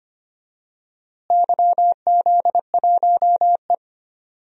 Morse code, heard as a keyed tone, about 25 wpm, 700 Hz. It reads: YZ1E
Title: DXCC-100 Call Signs - 4 Characters - Repeated 2x - 25wpm